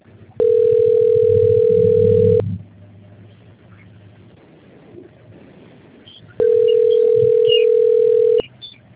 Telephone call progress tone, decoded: ringback tone

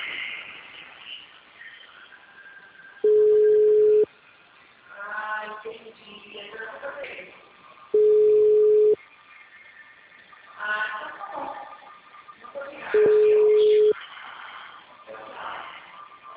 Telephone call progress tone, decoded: ringback tone

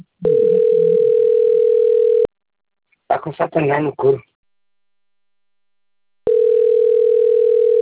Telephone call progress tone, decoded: ringback tone

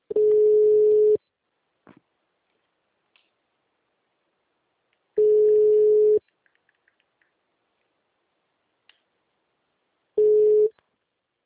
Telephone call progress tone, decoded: ringback tone

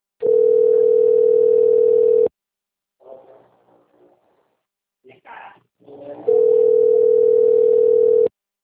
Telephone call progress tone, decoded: ringback tone

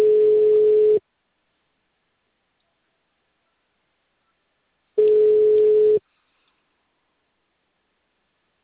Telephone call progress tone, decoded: ringback tone

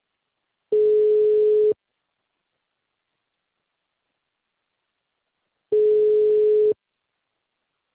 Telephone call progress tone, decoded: ringback tone

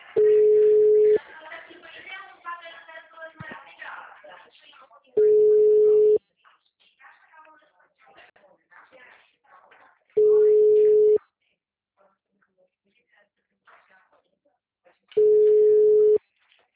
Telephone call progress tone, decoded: ringback tone

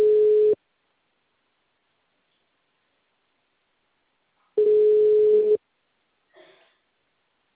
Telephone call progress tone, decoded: ringback tone

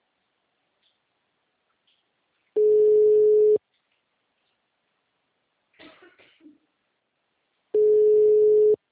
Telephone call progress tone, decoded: ringback tone

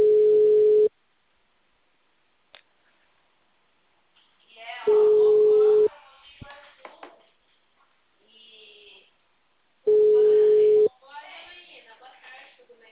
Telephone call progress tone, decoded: ringback tone

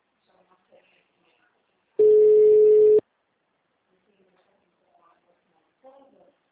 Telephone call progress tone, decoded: ringback tone

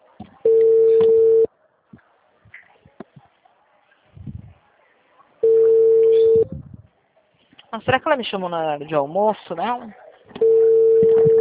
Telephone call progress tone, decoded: ringback tone

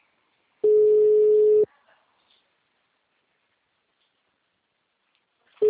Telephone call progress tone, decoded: ringback tone

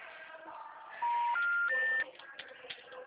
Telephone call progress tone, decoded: special information tone

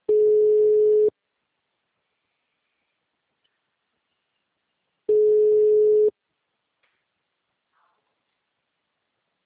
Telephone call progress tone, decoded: ringback tone

